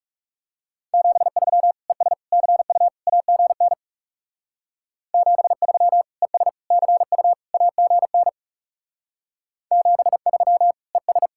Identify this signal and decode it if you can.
Morse code, heard as a keyed tone, about 35 wpm, 700 Hz. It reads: 73 ES CU AGN 73 ES CU AGN 73 ES
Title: Speed-Racing Common QSO Phrases Encore - 30wpm